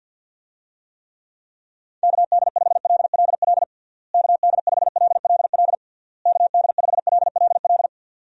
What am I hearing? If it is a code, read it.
Morse code, heard as a keyed tone, about 52 wpm, 700 Hz. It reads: KD5LLL KD5LLL KD5LLL
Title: USA Call Signs - 6 Characters - Repeated 3x - 50wpm